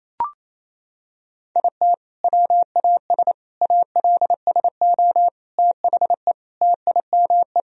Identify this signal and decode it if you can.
Morse code, heard as a keyed tone, about 28 wpm, 700 Hz. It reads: IT WAS ALSO THE TIME
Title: Sentences from Top 1000 Words - Review Part 1 - 28wpm